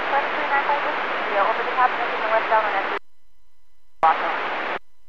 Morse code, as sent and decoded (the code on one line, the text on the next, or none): none